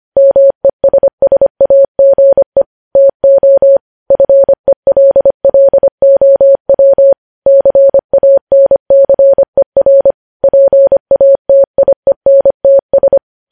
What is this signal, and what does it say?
Morse code, heard as a keyed tone, about 25 wpm, 550 Hz. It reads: MESSAGE TO FELLOW CANCER PATIENTS
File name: podcast / News headlines-1x-25x25